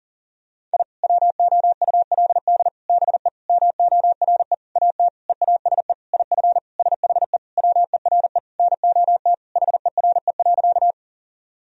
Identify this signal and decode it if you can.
Morse code, heard as a keyed tone, about 40 wpm, 700 Hz. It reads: I WOULD BE MORE AT EASE IF SHE WERE NOT HERE.
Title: Sentences from Top 1000 Words - Review Part 1 - 40wpm